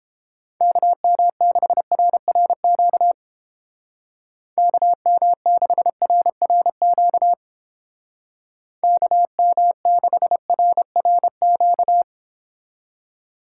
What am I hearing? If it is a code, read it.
Morse code, heard as a keyed tone, about 29 wpm, 700 Hz. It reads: KM6RRQ KM6RRQ KM6RRQ